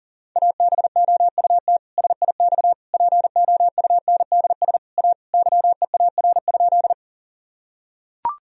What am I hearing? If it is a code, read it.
Morse code, heard as a keyed tone, about 37 wpm, 700 Hz. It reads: ABOUT SIX POUNDS A YEAR?
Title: Sentences from Top 1000 Words - Review Part 2 - 40wpm